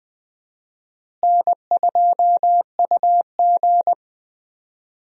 Morse code, heard as a keyed tone, about 20 wpm, 700 Hz. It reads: N2UG
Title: USA Call Signs - 4 Characters - Repeated 2x - 20wpm